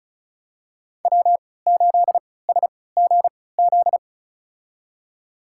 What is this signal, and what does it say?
Morse code, heard as a keyed tone, about 36 wpm, spaced as for 20 wpm, 700 Hz. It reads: W8SGZ
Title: USA Call Signs - 5 Characters - 3x Character Spacing - 35wpm